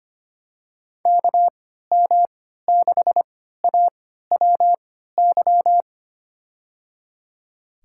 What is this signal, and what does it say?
Morse code, heard as a keyed tone, about 25 wpm, spaced as for 14 wpm, 700 Hz. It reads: KM6AWY
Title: USA Call Signs - 6 Characters - 3x Character Spacing - 25wpm